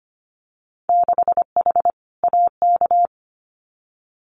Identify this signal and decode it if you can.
Morse code, heard as a keyed tone, about 25 wpm, 690 Hz. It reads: 6H AK
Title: Winter Field Day Exchanges Encore 25wpm